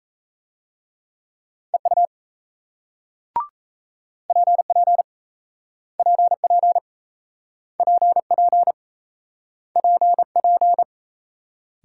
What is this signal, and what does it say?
Morse code, heard as a keyed tone, about 38 wpm, 700 Hz. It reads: EU PP PP PP PP